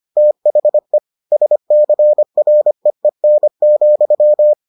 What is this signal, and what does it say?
Morse code, heard as a keyed tone, about 25 wpm, 600 Hz. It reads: THE SCREEN,